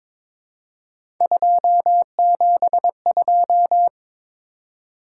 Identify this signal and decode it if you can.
Morse code, heard as a keyed tone, about 22 wpm, 700 Hz. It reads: 272
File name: practice / 22wpm / Three Letter-Number Combinations - 22wpm